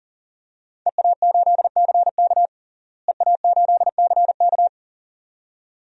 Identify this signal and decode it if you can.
Morse code, heard as a keyed tone, about 40 wpm, 700 Hz. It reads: EA8CK EA8CK
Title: DXCC-100 Call Signs - 5 Characters - Repeated 2x - 40wpm